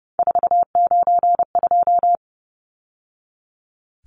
Morse code, heard as a keyed tone, about 30 wpm, 700 Hz. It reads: 492
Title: Numbers - 3-Digit 30wpm